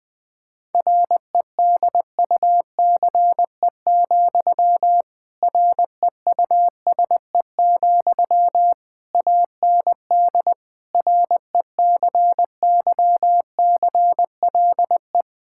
Morse code, heard as a keyed tone, about 20 wpm, 700 Hz. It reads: REDUCE, REUSE, AND RECYCLE
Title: Difficult Sentences with Hints - 20wpm